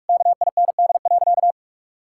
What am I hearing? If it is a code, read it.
Morse code, heard as a keyed tone, about 41 wpm, 700 Hz. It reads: KIND.
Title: Sentences from Top 1000 Words - Review Part 5 - 45wpm